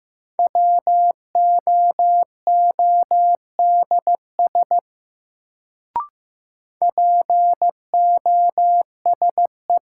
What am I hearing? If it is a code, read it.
Morse code, heard as a keyed tone, about 15 wpm, 700 Hz. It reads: WOODS POSE
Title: Sentences from Top 1000 Words - Review Part 2 - New Word Drill - 15wpm